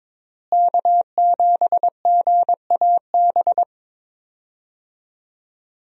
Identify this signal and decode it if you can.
Morse code, heard as a keyed tone, about 22 wpm, 700 Hz. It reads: K7GAB
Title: USA Call Signs - 5 Characters - Repeated 2x - 22wpm